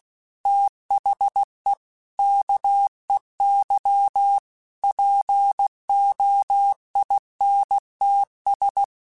Morse code, heard as a keyed tone, about 16 wpm, 800 Hz. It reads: THE KEY POINTS
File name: Technical Subjects / Technical Subjects-mp3s / Reactance @15